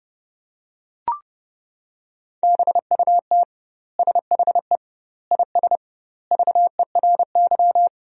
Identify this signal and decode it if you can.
Morse code, heard as a keyed tone, about 32 wpm, 700 Hz. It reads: BUT SHE IS VERY
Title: 2x Word-Spacing - Sentences from Top 400 Words - 30wpm